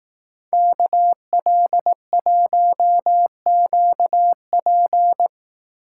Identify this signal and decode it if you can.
Morse code, heard as a keyed tone, about 18 wpm, 700 Hz. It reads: KL1QP
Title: Speed Racing - USA 2x2 Call Signs 15wpm